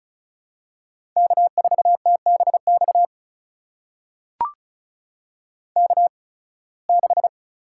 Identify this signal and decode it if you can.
Morse code, heard as a keyed tone, about 36 wpm, spaced as for 23 wpm, 700 Hz. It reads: K4TBX K6
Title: USA Call Signs - 8x Character Spacing - 35wpm